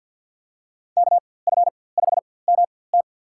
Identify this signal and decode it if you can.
Morse code, heard as a keyed tone, about 48 wpm, spaced as for 23 wpm, 700 Hz. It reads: KF5MT
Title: USA Call Signs - 5 Characters - 4x Character Spacing - 50wpm